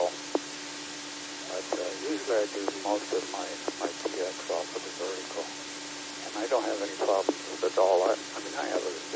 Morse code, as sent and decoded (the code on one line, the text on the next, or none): none